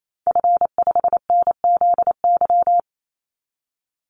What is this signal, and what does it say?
Morse code, heard as a keyed tone, about 28 wpm, 700 Hz. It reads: F5NZY
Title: Call Signs Repeated 2x - 28wpm